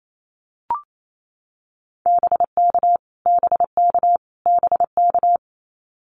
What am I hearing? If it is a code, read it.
Morse code, heard as a keyed tone, about 28 wpm, 700 Hz. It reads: BK BK BK
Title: Abbreviations Repeated 3x - 28wpm